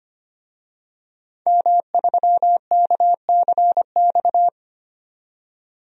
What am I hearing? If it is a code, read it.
Morse code, heard as a keyed tone, about 25 wpm, 700 Hz. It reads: M3KCX
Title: DXCC-100 Call Signs - 5 Characters - 25wpm